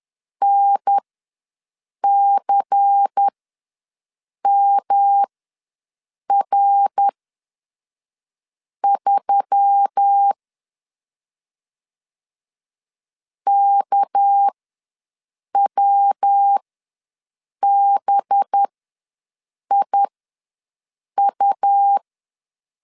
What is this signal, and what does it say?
Morse code, heard as a keyed tone, about 10 wpm, spaced as for 5 wpm, 800 Hz. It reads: NCMR3 KWBIU
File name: LEZIONI / 6 PKX0 / 6 PK ws 10wpm 3745